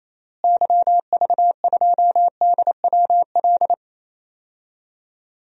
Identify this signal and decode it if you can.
Morse code, heard as a keyed tone, about 28 wpm, 700 Hz. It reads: YV2DWL
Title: DXCC-100 Call Signs - 6 Characters - Repeated 3x - 28wpm